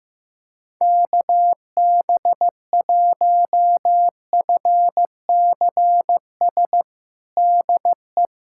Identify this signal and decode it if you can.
Morse code, heard as a keyed tone, about 16 wpm, 700 Hz. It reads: KB1FCS DE
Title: USA Callsign DE Callsign 15wpm